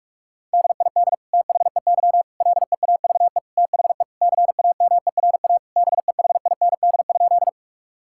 Morse code, heard as a keyed tone, about 46 wpm, 700 Hz. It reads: DID THEY LEAVE THE CAMERA BEHIND?
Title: Sentences from Top 1000 Words - Review Part 5 - 45wpm